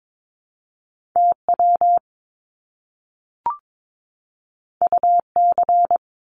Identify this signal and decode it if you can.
Morse code, heard as a keyed tone, about 22 wpm, 700 Hz. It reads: TW UC
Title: Top Two Letter Combinations in Words 22wpm